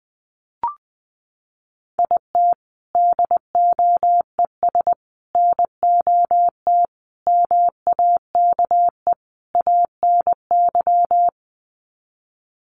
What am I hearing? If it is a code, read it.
Morse code, heard as a keyed tone, about 20 wpm, 700 Hz. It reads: IT DOES NOT MAKE ANY
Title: Sets of 5 Words Encore2 20wpm